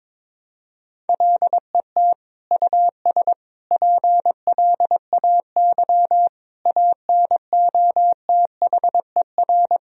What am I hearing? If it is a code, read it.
Morse code, heard as a keyed tone, about 22 wpm, 700 Hz. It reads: LET US PLAY ANOTHER
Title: Sentences from Top 1000 Words - Review Part 3 - 22wpm